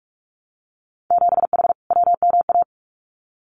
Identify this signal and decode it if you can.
Morse code, heard as a keyed tone, about 46 wpm, 700 Hz. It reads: 7H WMA
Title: Winter Field Day Exchanges Encore 45wpm